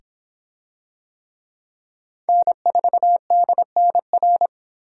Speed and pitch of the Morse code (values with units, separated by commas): 26 wpm, 700 Hz